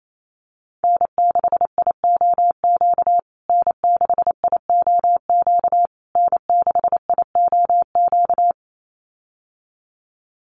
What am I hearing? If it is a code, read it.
Morse code, heard as a keyed tone, about 28 wpm, 700 Hz. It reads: N6IOQ N6IOQ N6IOQ
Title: USA Call Signs Repeated 3x 28wpm